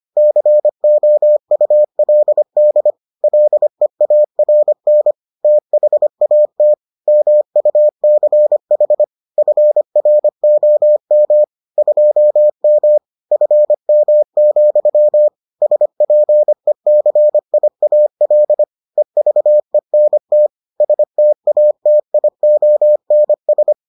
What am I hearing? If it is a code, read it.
Morse code, heard as a keyed tone, about 25 wpm, 600 Hz. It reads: COULD LEARN THAT MUCH FROM 2M FM, SPECIAL EVENT STATIONS